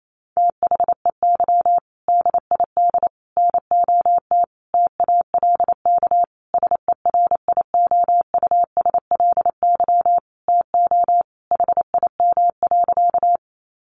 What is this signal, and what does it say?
Morse code, heard as a keyed tone, about 28 wpm, 700 Hz. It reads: THEY DID NOT TALK SERIOUSLY TO HIM.
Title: Sentences from Top 1200 Words 28wpm